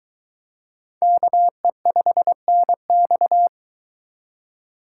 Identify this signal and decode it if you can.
Morse code, heard as a keyed tone, about 22 wpm, 700 Hz. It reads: KE5NX